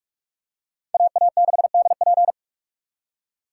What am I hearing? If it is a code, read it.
Morse code, heard as a keyed tone, about 40 wpm, 700 Hz. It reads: AA6DP